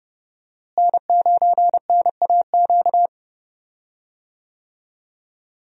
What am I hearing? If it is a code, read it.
Morse code, heard as a keyed tone, about 30 wpm, 700 Hz. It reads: N9NAQ
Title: USA Call Signs - 5 Characters - Repeated 3x - 30wpm